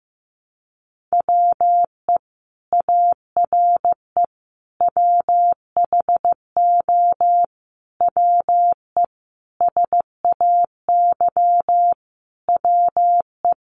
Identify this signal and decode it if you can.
Morse code, heard as a keyed tone, about 15 wpm, 700 Hz. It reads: WE ARE WHO WE SAY WE
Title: Sentences from Top 200 Words 15wpm